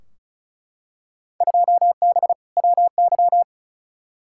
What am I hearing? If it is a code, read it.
Morse code, heard as a keyed tone, about 34 wpm, 700 Hz. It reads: 2B WY
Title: Field Day Exchanges Encore - 35wpm